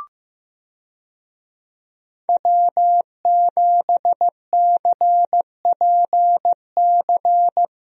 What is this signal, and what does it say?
Morse code, heard as a keyed tone, about 16 wpm, 700 Hz. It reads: W7CPC
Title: USA Call Signs - 5 Characters - Repeated 2x - 15wpm